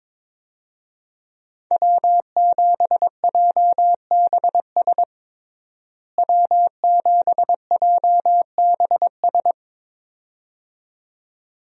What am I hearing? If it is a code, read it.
Morse code, heard as a keyed tone, about 22 wpm, 700 Hz. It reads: W7JBS W7JBS
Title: USA Call Signs - 5 Characters - Repeated 2x - 22wpm